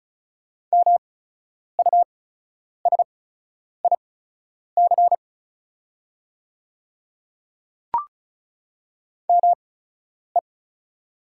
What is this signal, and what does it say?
Morse code, heard as a keyed tone, about 36 wpm, spaced as for 10 wpm, 690 Hz. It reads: MUSIC ME
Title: 8x Character Spacing - Top 300 Words - 35wpm